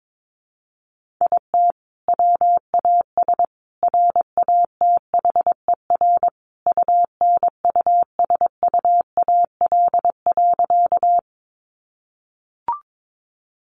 Morse code, heard as a keyed tone, about 22 wpm, 700 Hz. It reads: IT WAS RATHER UNUSUAL.
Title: Sentences from Top 1400 Words 22wpm